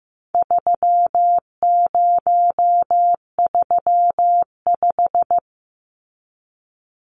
Morse code, heard as a keyed tone, about 15 wpm, 700 Hz. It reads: 3035
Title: Numbers - 4-Digit 15wpm